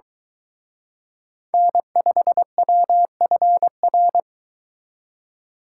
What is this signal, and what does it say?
Morse code, heard as a keyed tone, about 23 wpm, 700 Hz. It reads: N5WFR